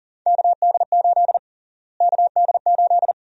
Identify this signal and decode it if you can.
Morse code, heard as a keyed tone, about 40 wpm, 700 Hz. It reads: KD8 KD8
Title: USA Call Sign Prefixes - Repeated 2x - 40wpm